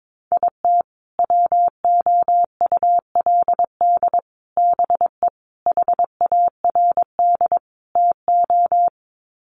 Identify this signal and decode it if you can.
Morse code, heard as a keyed tone, about 22 wpm, 700 Hz. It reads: IT WOULD BE HARD TO
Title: Sets of 5 Words Encore2 22wpm